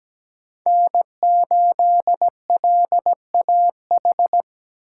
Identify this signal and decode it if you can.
Morse code, heard as a keyed tone, about 17 wpm, 700 Hz. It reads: N8LAH
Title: USA Call Signs - 5 Characters - 3x Character Spacing - 17wpm